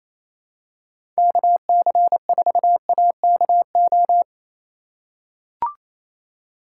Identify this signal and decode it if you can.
Morse code, heard as a keyed tone, about 28 wpm, 700 Hz. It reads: KC4AKO
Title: USA Call Signs - 6 Characters - 3x Character Spacing - 28wpm